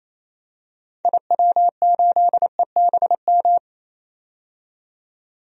Morse code, heard as a keyed tone, about 28 wpm, 700 Hz. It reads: IW8EBM
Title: DXCC-100 Call Signs - 6 Characters - 28wpm